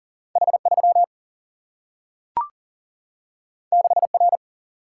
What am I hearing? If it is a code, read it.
Morse code, heard as a keyed tone, about 40 wpm, 700 Hz. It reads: H 3 6 R
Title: Single-Letter Single-Number 40wpm